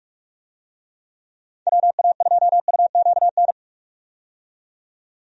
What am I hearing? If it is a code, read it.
Morse code, heard as a keyed tone, about 43 wpm, 700 Hz. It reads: WA2UQN
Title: USA Call Signs - 6 Characters - 45wpm